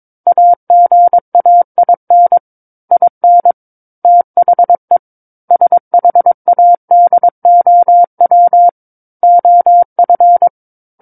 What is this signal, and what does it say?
Morse code, heard as a keyed tone, about 22 wpm, 700 Hz. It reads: AGAIN IN THE SHADOW OF
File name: Book One / 13-Chapter 13 - 22 WPM